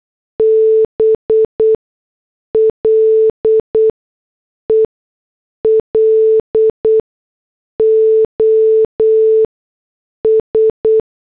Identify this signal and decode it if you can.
Morse code, heard as a keyed tone, about 8 wpm, spaced as for 6 wpm, 440 Hz. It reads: BLE LOS